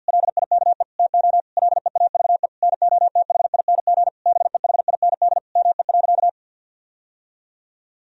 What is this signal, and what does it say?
Morse code, heard as a keyed tone, about 51 wpm, 700 Hz. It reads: LIKE TO LEAVE NOTHING BEHIND ME.
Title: Sentences from Top 1000 Words - Review Part 1 - 50wpm